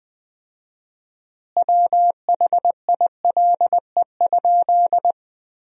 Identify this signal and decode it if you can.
Morse code, heard as a keyed tone, about 20 wpm, 700 Hz. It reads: WHILE?
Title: 4x Word-Spacing - Sentences from Top 300 Words - 20wpm